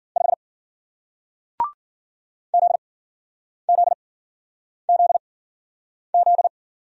Morse code, heard as a keyed tone, about 49 wpm, spaced as for 9 wpm, 700 Hz. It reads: 5 ZZZZ
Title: Speed-Racing Single Letter-Number 40wpm